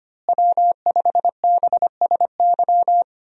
Morse code, heard as a keyed tone, about 25 wpm, 700 Hz. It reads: W5BSY
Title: USA Call Signs - 5 Characters - Repeated 2x - 25wpm